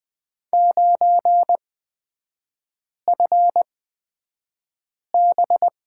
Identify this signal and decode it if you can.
Morse code, heard as a keyed tone, about 20 wpm, spaced as for 5 wpm, 690 Hz. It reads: 9FB
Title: 229 Lesson 37e - Letter x - Three Characters 20wpm